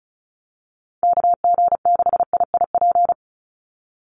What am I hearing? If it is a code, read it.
Morse code, heard as a keyed tone, about 34 wpm, 700 Hz. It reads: KG6IIP
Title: Call Signs - 35wpm